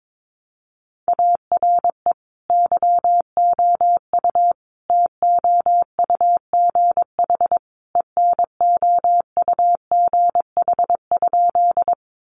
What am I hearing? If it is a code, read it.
Morse code, heard as a keyed tone, about 22 wpm, 700 Hz. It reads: ARE YOU TOUGH ENOUGH?
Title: Sentences from Top 1200 Words 22wpm